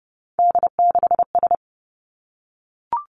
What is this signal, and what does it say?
Morse code, heard as a keyed tone, about 30 wpm, 700 Hz. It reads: D6S
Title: Three Challenging Letter-Number Combinations 30wpm